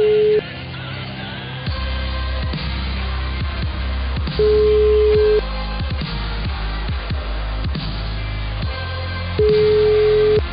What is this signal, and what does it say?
Ringback tone (Europe, 425 Hz), with noise.